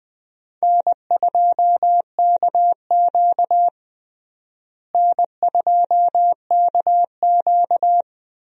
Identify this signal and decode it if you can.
Morse code, heard as a keyed tone, about 20 wpm, 700 Hz. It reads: N2KQ N2KQ
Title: USA Call Signs - 4 Characters - Repeated 2x - 20wpm